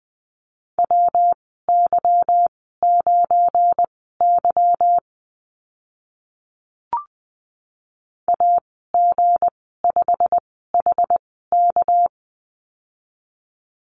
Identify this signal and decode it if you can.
Morse code, heard as a keyed tone, about 20 wpm, spaced as for 14 wpm, 700 Hz. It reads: WY9Y AG5HK
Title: 238 Lesson 38g - Letter q - Call Signs 20wpm